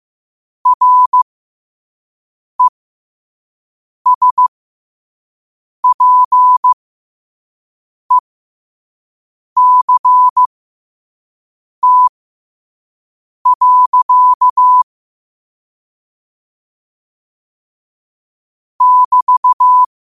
Morse code, heard as a keyed tone, about 15 wpm, spaced as for 5 wpm, 1000 Hz. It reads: RESPECT. <BT>
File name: qotdfiles / 5wpm